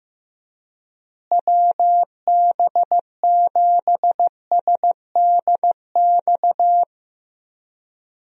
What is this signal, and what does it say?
Morse code, heard as a keyed tone, about 15 wpm, 700 Hz. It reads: WB7SDX
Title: USA Call Signs - 6 Characters - Repeated 3x - 15wpm